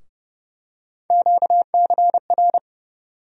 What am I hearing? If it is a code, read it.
Morse code, heard as a keyed tone, about 30 wpm, 700 Hz. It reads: QCR
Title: USA Call Sign Postfixes - 3 Letters 30wpm